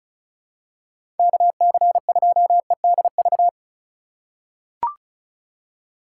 Morse code, heard as a keyed tone, about 36 wpm, 700 Hz. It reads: KC2EDV
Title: USA Call Signs - 6 Characters - 4x Character Spacing - 35wpm